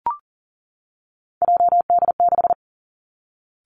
Morse code, heard as a keyed tone, about 38 wpm, 700 Hz. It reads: JD6
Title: Three Challenging Letter-Number Combinations 40wpm